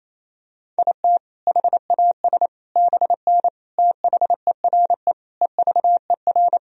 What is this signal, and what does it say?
Morse code, heard as a keyed tone, about 28 wpm, 700 Hz. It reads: IT HAS BN THERE EVER
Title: Sentences from Top 400 Words and Abbreviations 28wpm